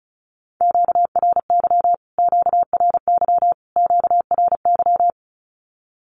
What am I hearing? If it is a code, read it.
Morse code, heard as a keyed tone, about 34 wpm, 700 Hz. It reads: QRY QRY QRY
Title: Q-Codes Repeated 3x 35wpm